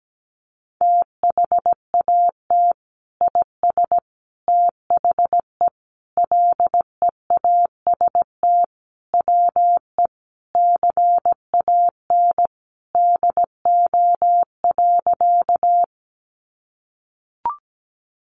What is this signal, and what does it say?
Morse code, heard as a keyed tone, about 17 wpm, 700 Hz. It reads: THAT IS THE LEAST WE CAN DO.
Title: Sentences from Top 700 Words 17wpm